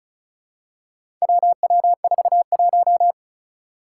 Morse code, heard as a keyed tone, about 34 wpm, 700 Hz. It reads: WW41
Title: Four Challenging Letter-Number Combinations 35wpm